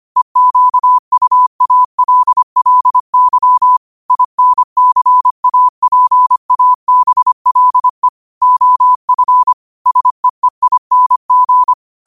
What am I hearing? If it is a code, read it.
Morse code, heard as a keyed tone, about 25 wpm, 1000 Hz. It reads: EQUALLY INCAPABLE OF SEEING
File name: qotdfiles / 25wpm